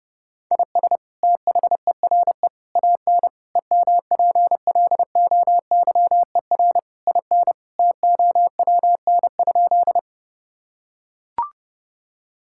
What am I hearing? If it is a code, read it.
Morse code, heard as a keyed tone, about 30 wpm, 700 Hz. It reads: IS THERE AN EMPLOYER IN TOWN?
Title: Sentences from Top 1600 Words 30wpm